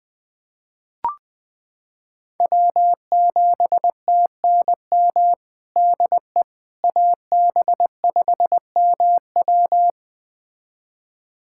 Morse code, heard as a keyed tone, about 21 wpm, 700 Hz. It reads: W7TNM DE AB5MW
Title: USA Callsign DE Callsign - 5 Characters - 20wpm